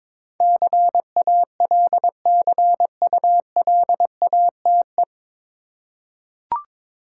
Morse code, heard as a keyed tone, about 22 wpm, 700 Hz. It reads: CALCULATE
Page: Morse Code Ninja practice 22wpm